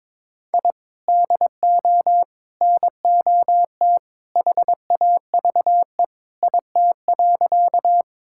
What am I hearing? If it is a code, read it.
Morse code, heard as a keyed tone, about 22 wpm, 700 Hz. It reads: I DO NOT HAVE IT.
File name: practice / 22wpm / Sentences with 2-Word Phrases - 22wpm